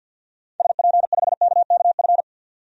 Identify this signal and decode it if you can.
Morse code, heard as a keyed tone, about 52 wpm, 700 Hz. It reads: SP5KKF
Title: DXCC-100 Call Signs - 6 Characters - 50wpm